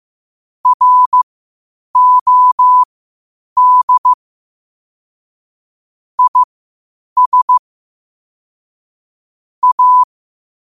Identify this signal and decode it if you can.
Morse code, heard as a keyed tone, about 15 wpm, spaced as for 8 wpm, 1000 Hz. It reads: ROD IS A